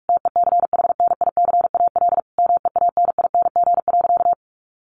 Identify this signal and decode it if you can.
Morse code, heard as a keyed tone, about 46 wpm, 700 Hz. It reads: TECHNICAL MEANING.
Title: Sentences from Top 1100 Words 45wpm